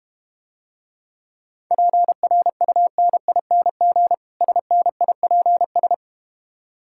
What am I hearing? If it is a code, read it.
Morse code, heard as a keyed tone, about 32 wpm, 700 Hz. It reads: PRUNING SNIPS